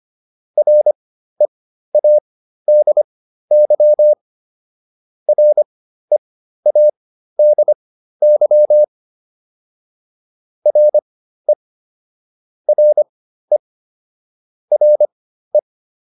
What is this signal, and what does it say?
Morse code, heard as a keyed tone, about 25 wpm, spaced as for 13 wpm, 600 Hz. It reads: READY READY RE RE RE